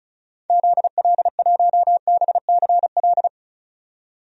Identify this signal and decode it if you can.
Morse code, heard as a keyed tone, about 34 wpm, 700 Hz. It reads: ZL1BCL
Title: DXCC-100 Call Signs - 6 Characters - 35wpm